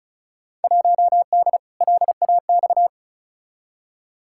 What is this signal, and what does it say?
Morse code, heard as a keyed tone, about 36 wpm, 700 Hz. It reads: 1D LAX
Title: Field Day Exchanges - 35wpm